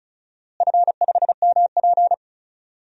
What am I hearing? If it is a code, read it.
Morse code, heard as a keyed tone, about 36 wpm, 700 Hz. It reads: F5MP
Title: DXCC-100 Call Signs - 4 Characters - Repeated 2x - 35wpm